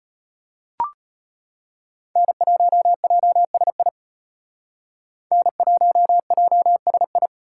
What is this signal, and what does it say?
Morse code, heard as a keyed tone, about 32 wpm, 700 Hz. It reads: N1JSI N1JSI